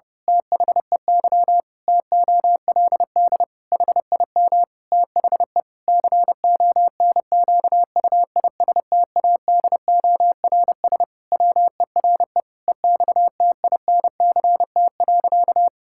Morse code, heard as a keyed tone, about 30 wpm, 700 Hz. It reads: THEY TOLD HIM THE CONQUISTADORS WERE EXTINCT.
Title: Pangram Challenge v2 - 30wpm